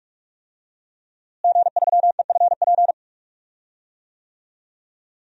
Morse code, heard as a keyed tone, about 42 wpm, 700 Hz. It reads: G3EFP
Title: DXCC-100 Call Signs - 5 Characters - Repeated 3x - 45wpm